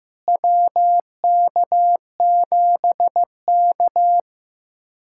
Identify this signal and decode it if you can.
Morse code, heard as a keyed tone, about 15 wpm, 700 Hz. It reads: WK7K
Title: USA Callsign DE Callsign - 4 Characters - 15wpm